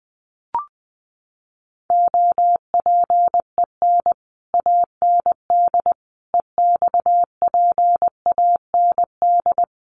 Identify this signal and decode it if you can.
Morse code, heard as a keyed tone, about 20 wpm, 700 Hz. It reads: OPEN AND EXPAND
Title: Sentences from Top 1400 Words 20wpm